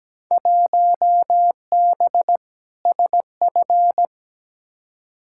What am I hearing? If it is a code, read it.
Morse code, heard as a keyed tone, about 17 wpm, 700 Hz. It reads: 1B SF
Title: Field Day Exchanges Encore - 17wpm